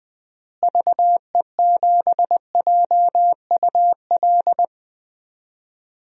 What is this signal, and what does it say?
Morse code, heard as a keyed tone, about 20 wpm, 700 Hz. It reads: VE7JUL
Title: DXCC-100 Call Signs - 6 Characters - Repeated 3x - 20wpm